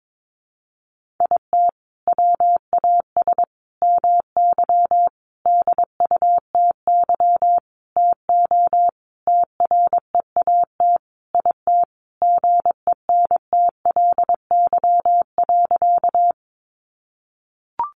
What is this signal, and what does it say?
Morse code, heard as a keyed tone, about 22 wpm, 700 Hz. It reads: IT WAS MY DUTY TO TREAT IT GENTLY.
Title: Sentences from Top 1500 Words 22wpm